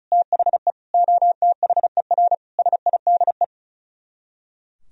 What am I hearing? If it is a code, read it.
Morse code, heard as a keyed tone, about 45 wpm, 700 Hz. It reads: THE OTHER SIDE